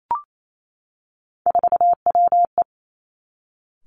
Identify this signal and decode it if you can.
Morse code, heard as a keyed tone, about 28 wpm, 700 Hz. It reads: 4WE
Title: Three Challenging Letter-Number Combinations 28wpm